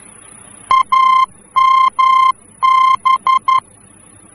Morse code, heard as a keyed tone, about 11 wpm, 1050 Hz. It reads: AMB
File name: france / mx / AMB-F-VOR